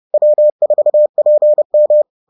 Morse code, heard as a keyed tone, about 30 wpm, 600 Hz. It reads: W4PM